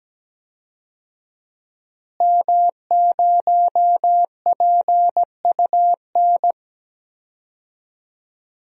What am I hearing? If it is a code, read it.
Morse code, heard as a keyed tone, about 17 wpm, 700 Hz. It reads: M0PUN
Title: DXCC-100 Call Signs - 5 Characters - Repeated 2x - 17wpm